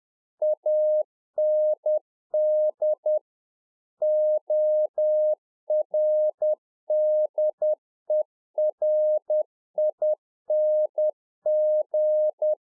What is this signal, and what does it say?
Morse code, heard as a keyed tone, about 10 wpm, 610 Hz. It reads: AND ORDERING
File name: 2011-04-mp3 / Czech Army Morse Keys 10WPM